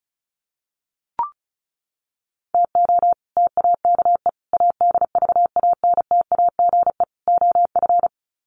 Morse code, heard as a keyed tone, about 34 wpm, 700 Hz. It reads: TO TAKE ADVANTAGE OF
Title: Sets of 4 Words 35wpm